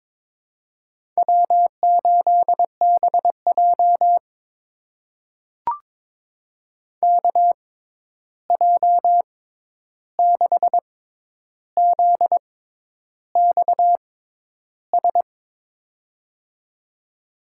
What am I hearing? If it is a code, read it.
Morse code, heard as a keyed tone, about 22 wpm, spaced as for 10 wpm, 700 Hz. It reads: W8BJ KJ6ZXS
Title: USA Call Signs - 6x Character Spacing - 22wpm